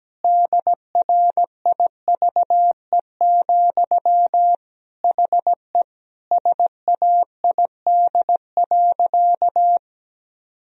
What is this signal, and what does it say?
Morse code, heard as a keyed tone, about 17 wpm, 700 Hz. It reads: DRIVE, HE SAID.
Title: Sentences from Top 1000 Words - Review Part 2 - 17wpm